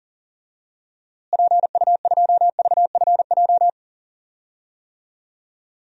Morse code, heard as a keyed tone, about 40 wpm, 700 Hz. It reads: PU2VFJ
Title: DXCC-100 Call Signs - 6 Characters - Repeated 2x - 40wpm